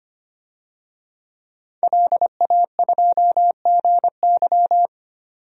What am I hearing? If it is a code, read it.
Morse code, heard as a keyed tone, about 25 wpm, 700 Hz. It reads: LA2GY
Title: DXCC-100 Call Signs - 5 Characters - 25wpm